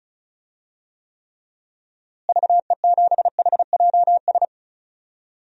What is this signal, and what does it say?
Morse code, heard as a keyed tone, about 36 wpm, 700 Hz. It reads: VE7HJS
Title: DXCC-100 Call Signs - 6 Characters - Repeated 2x - 35wpm